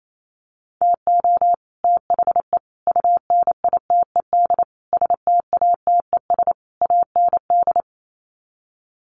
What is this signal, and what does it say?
Morse code, heard as a keyed tone, about 28 wpm, 700 Hz. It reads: TO THE UNITED STATES AND
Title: Sets of 5 Words Encore 28wpm